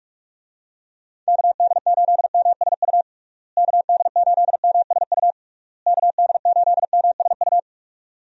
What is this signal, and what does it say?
Morse code, heard as a keyed tone, about 47 wpm, 700 Hz. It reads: KD8MSU KD8MSU KD8MSU
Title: USA Call Signs - 6 Characters - Repeated 3x - 45wpm